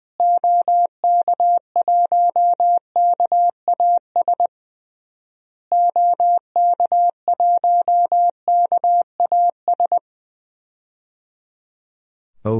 Morse code, heard as a keyed tone, about 20 wpm, 700 Hz. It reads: OK1KAS OK1KAS
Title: DXCC-100 Call Signs - 6 Characters - Repeated 3x - 20wpm